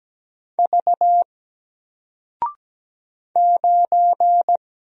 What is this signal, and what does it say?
Morse code, heard as a keyed tone, about 17 wpm, 700 Hz. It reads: V 9
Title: Single Character (Letter:Number:Punctuation:Prosign) Alt 17wpm